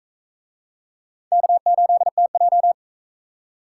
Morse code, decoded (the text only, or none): K8TJ